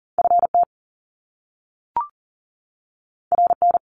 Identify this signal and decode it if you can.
Morse code, heard as a keyed tone, about 40 wpm, 700 Hz. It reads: FT RN
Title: Top Two Letter Combinations in Words 40wpm